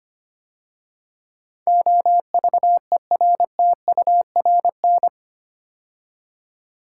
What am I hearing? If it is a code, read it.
Morse code, heard as a keyed tone, about 25 wpm, 700 Hz. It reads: OVERTURN
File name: 8 Letter Words - First 3 Letters Given - 25wpm